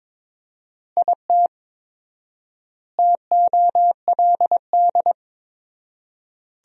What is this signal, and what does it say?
Morse code, heard as a keyed tone, about 22 wpm, 700 Hz. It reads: IT TOLD
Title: 4x Word-Spacing - Sentences from Top 400 Words - 22wpm